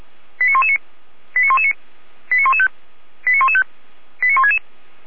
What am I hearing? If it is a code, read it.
Five-tone selective call (ZVEI), heard as quick runs of five tones, tones of about 70 ms. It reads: 89119 89119 89115 89115 89166